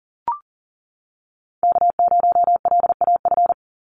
Morse code, heard as a keyed tone, about 40 wpm, 700 Hz. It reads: K0LAF
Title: NA SOTA Chaser Callsigns 40wpm